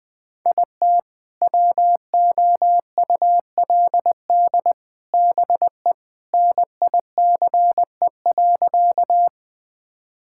Morse code, heard as a keyed tone, about 20 wpm, 700 Hz. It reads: IT WOULD BE NICE.
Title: Sentences with 3-Word Phrases - Spoken Beginning - 20wpm